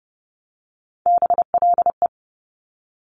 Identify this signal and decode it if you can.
Morse code, heard as a keyed tone, about 30 wpm, 700 Hz. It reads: BLE
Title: Top Three Letter Combinations in Words 30wpm